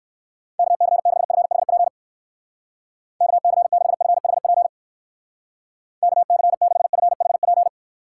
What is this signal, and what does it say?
Morse code, heard as a keyed tone, about 58 wpm, 700 Hz. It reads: KC6FHP KC6FHP KC6FHP